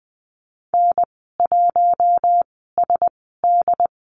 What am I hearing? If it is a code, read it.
Morse code, heard as a keyed tone, about 20 wpm, spaced as for 15 wpm, 700 Hz. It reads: N1SD
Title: 069 Lesson 12g - Letter d - Call Signs 20wpm